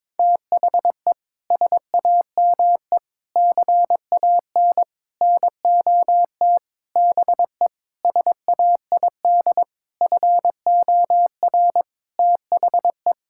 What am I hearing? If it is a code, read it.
Morse code, heard as a keyed tone, about 22 wpm, 700 Hz. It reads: THE SAME CAN NOT BE SAID FOR THE
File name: practice / 22wpm / Sentences with 8-Word Phrases - 22wpm